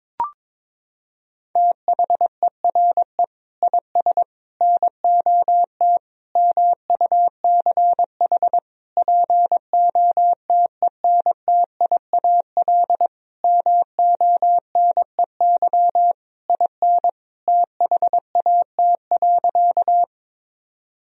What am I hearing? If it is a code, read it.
Morse code, heard as a keyed tone, about 22 wpm, 700 Hz. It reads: THERE IS NOT MUCH POTENTIAL MONEY IN THAT.
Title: Sentences from Top 1000 Words - Review Part 4 - 22wpm